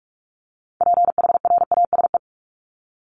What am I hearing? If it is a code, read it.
Morse code, heard as a keyed tone, about 44 wpm, 700 Hz. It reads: PHRASE
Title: Top 1501-1750 Words 45wpm